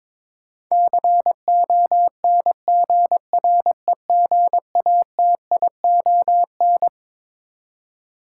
Morse code, heard as a keyed tone, about 22 wpm, 700 Hz. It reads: CONGREGATION
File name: Twelve Letter Words - 22wpm